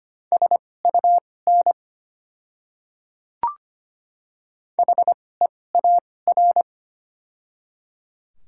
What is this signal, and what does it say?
Morse code, heard as a keyed tone, about 25 wpm, spaced as for 18 wpm, 700 Hz. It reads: SUN HEAR
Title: 2x Character Spacing - Top 300 Words - 25wpm